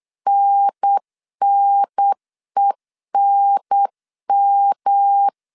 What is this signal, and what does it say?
Morse code, heard as a keyed tone, about 8 wpm, 800 Hz. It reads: NNENM